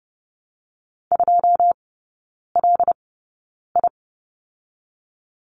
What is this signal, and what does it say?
Morse code, heard as a keyed tone, about 30 wpm, 700 Hz. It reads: 2 L I
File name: Lesson 19e - Letter w - Three Characters 30wpm